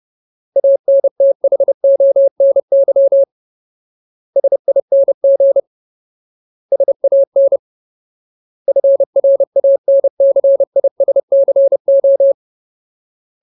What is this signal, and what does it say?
Morse code, heard as a keyed tone, about 30 wpm, 550 Hz. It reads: ANTHONY SING SAN FRANCISCO
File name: Ultimate Word List / 30WPM Ultimate